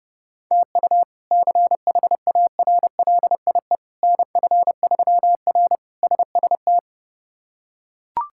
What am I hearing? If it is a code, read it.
Morse code, heard as a keyed tone, about 30 wpm, 700 Hz. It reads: TU CHARLIE NF3R SST
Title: K1USN SST QSO Practice - 30wpm